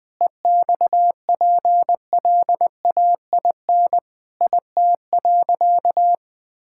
Morse code, heard as a keyed tone, about 21 wpm, 700 Hz. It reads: EXPLAIN IT.
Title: Sentences with 8-Word Phrases - Spoken Beginning - 20wpm